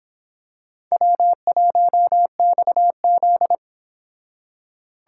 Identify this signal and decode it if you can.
Morse code, heard as a keyed tone, about 26 wpm, 700 Hz. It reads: W1XZ